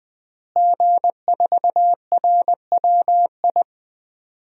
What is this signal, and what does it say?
Morse code, heard as a keyed tone, about 20 wpm, 700 Hz. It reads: G4RWI
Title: DXCC-100 Call Signs - 5 Characters - Repeated 2x - 20wpm